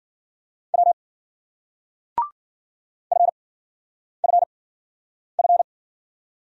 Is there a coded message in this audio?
Yes: Morse code from 0.7 to 5.6 s, heard as a keyed tone, about 55 wpm, spaced as for 9 wpm, 700 Hz. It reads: W FFF